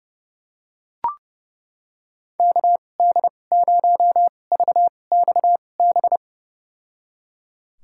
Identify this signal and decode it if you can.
Morse code, heard as a keyed tone, about 29 wpm, spaced as for 22 wpm, 700 Hz. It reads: KD0VXB
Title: USA Call Signs - 6 Characters - 2x Character Spacing - 30wpm